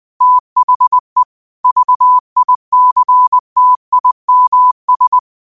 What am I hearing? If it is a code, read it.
Morse code, heard as a keyed tone, about 20 wpm, 1000 Hz. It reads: THE VICTIMS
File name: qotdfiles / 20wpm